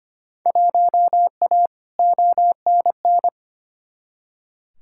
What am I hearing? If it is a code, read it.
Morse code, heard as a keyed tone, about 25 wpm, 700 Hz. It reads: 1A ONN
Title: Field Day Exchanges Encore - 25wpm